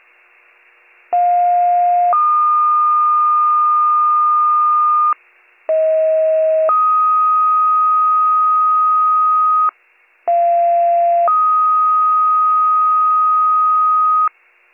Two-tone page, heard as a one-second tone, then a three-second tone, about 710 then 1220 Hz, about 640 then 1250 Hz, about 690 then 1250 Hz. It is heard from 1.1 to 14.3 s.